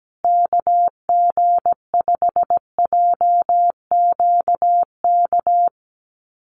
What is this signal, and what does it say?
Morse code, heard as a keyed tone, about 17 wpm, 700 Hz. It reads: KG5JQK
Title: USA Call Signs Repeated 2x 17wpm